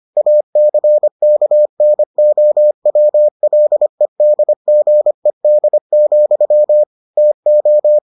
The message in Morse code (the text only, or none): ACKNOWLEDGED, TO